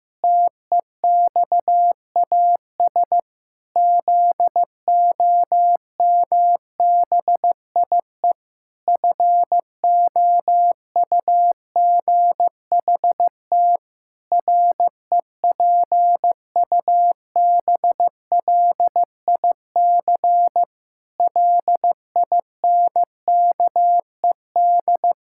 Morse code, heard as a keyed tone, about 15 wpm, 700 Hz. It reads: TEXAS ZOMBIE FOUGHT REPUBLIC LINKED
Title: Pangram Challenge v2 - 15wpm